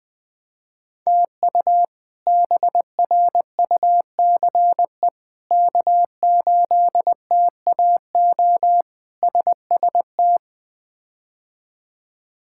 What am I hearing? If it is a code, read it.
Morse code, heard as a keyed tone, about 20 wpm, 700 Hz. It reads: TU BRUCE K8TAO SST
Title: K1USN SST QSO Practice - 20wpm